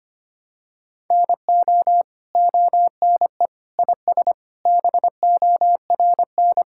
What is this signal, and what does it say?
Morse code, heard as a keyed tone, about 25 wpm, 700 Hz. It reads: NO ONE IS BORN
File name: Sets of 9 Words - 25wpm